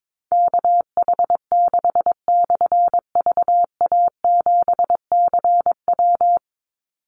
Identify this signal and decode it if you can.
Morse code, heard as a keyed tone, about 22 wpm, 700 Hz. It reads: KH6/VA7CW
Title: Call Signs - 22wpm